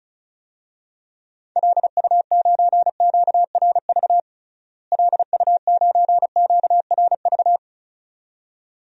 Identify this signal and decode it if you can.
Morse code, heard as a keyed tone, about 36 wpm, 700 Hz. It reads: LU9QRV LU9QRV
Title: DXCC-100 Call Signs - 6 Characters - Repeated 2x - 35wpm